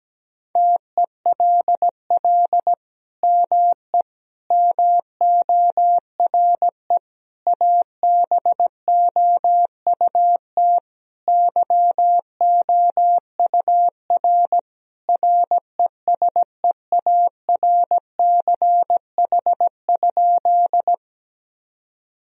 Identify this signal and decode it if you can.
Morse code, heard as a keyed tone, about 17 wpm, 700 Hz. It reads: TELL ME MORE ABOUT YOUR RESEARCH?
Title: Sentences from Top 1000 Words - Review Part 4 - 17wpm